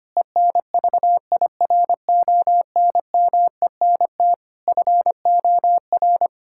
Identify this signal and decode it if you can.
Morse code, heard as a keyed tone, about 25 wpm, 700 Hz. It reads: ENVIRONMENT FOR
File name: practice / 25wpm / Difficult Sentences with Hints - 25wpm